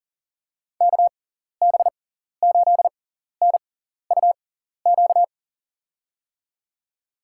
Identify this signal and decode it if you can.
Morse code, heard as a keyed tone, about 40 wpm, spaced as for 14 wpm, 700 Hz. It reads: KB8NUQ
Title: USA Call Signs - 6 Characters - 6x Character Spacing - 40wpm